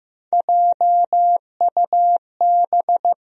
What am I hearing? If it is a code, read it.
Morse code, heard as a keyed tone, about 15 wpm, 700 Hz. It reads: JUB
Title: USA Call Sign Postfixes - 3 Letters 15wpm